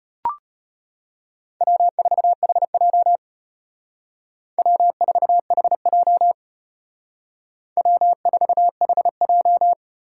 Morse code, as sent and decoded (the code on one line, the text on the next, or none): .-- ....- .... .--- / .-- ....- .... .--- / .-- ....- .... .---
W4HJ W4HJ W4HJ